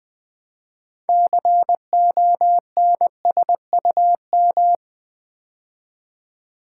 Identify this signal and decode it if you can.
Morse code, heard as a keyed tone, about 20 wpm, 700 Hz. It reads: CONSUM
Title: Build A Word - 2 Letters At A Time - 8-Letter Words - 20wpm